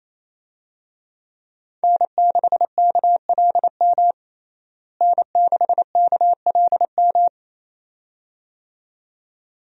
Morse code, heard as a keyed tone, about 28 wpm, 700 Hz. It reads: N6KLM N6KLM
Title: USA Call Signs - 5 Characters - Repeated 2x - 28wpm